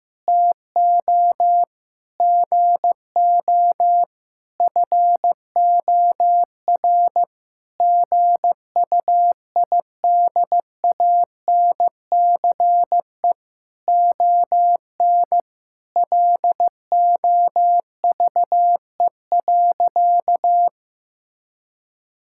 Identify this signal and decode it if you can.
Morse code, heard as a keyed tone, about 15 wpm, 700 Hz. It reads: TO GO FOR GUIDANCE ON LOVE.
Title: Sentences from Top 1600 Words 15wpm